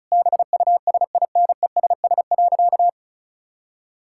Morse code, heard as a keyed tone, about 36 wpm, 700 Hz. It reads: BUSINESS.
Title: Sentences from Top 1000 Words - Review Part 6 - 35wpm